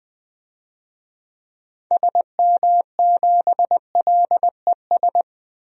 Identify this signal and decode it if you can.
Morse code, heard as a keyed tone, about 21 wpm, 700 Hz. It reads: SM7LES
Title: DXCC-100 Call Signs - 6 Characters - Repeated 2x - 20wpm